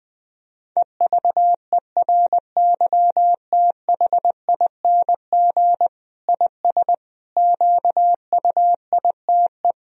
Morse code, heard as a keyed tone, about 20 wpm, 700 Hz. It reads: EVERYTHING IS QUITE
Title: Sentences from Top 1000 Words - Review Part 2 - 20wpm